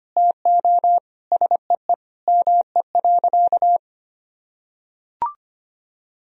Morse code, heard as a keyed tone, about 25 wpm, 700 Hz. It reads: TO SEE ME.
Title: Sentences from Top 1000 Words - Review Part 1 - 25wpm